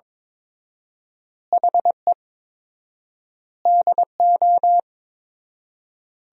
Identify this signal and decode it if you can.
Morse code, heard as a keyed tone, about 22 wpm, 700 Hz. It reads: HE DO
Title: 4x Word-Spacing - Sentences from Top 100 Words - 22wpm